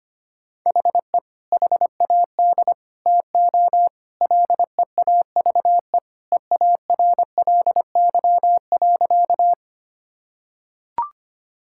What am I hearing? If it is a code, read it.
Morse code, heard as a keyed tone, about 25 wpm, 700 Hz. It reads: HE HAD TO LEAVE EARLY.
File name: practice / 25wpm / Sentences with 2-Word Phrases - 25wpm